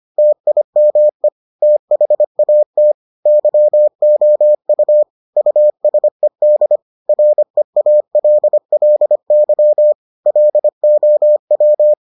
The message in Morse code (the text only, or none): TIME THAT YOU USED REALLY LOW